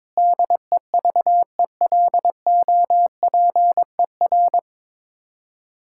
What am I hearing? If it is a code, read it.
Morse code, heard as a keyed tone, about 22 wpm, 700 Hz. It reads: DEVELOPER
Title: Practice QSO - Exchange 3 - More Info - 22wpm